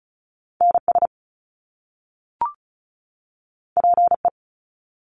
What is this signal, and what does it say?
Morse code, heard as a keyed tone, about 36 wpm, 700 Hz. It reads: NS PE
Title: Canadian Provinces and Territories 35wpm